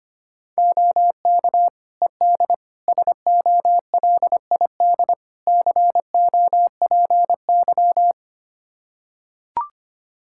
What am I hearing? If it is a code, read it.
Morse code, heard as a keyed tone, about 25 wpm, 700 Hz. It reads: OK ED SOLID COPY
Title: Practice QSO - Exchange 3 - More Info - 25wpm